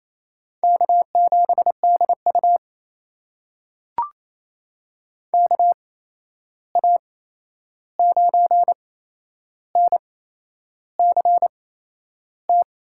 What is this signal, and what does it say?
Morse code, heard as a keyed tone, about 28 wpm, spaced as for 11 wpm, 700 Hz. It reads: K7DU KA9NCT
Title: USA Call Signs - 8x Character Spacing - 28wpm